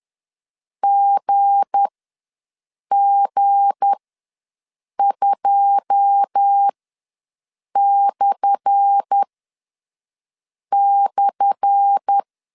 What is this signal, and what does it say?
Morse code, heard as a keyed tone, about 10 wpm, spaced as for 5 wpm, 800 Hz. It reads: GG2//
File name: LEZIONI / 10 SIGNS / 10C DOT ws 10wpm 4213